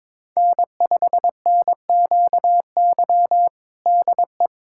Morse code, heard as a keyed tone, about 22 wpm, 700 Hz. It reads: N5NQY DE
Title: USA Callsign DE Callsign - 5 Characters - 22wpm